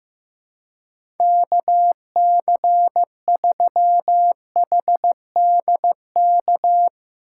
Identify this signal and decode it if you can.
Morse code, heard as a keyed tone, about 15 wpm, 700 Hz. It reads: KC3HDK
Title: USA Call Signs - 6 Characters - 4x Character Spacing - 15wpm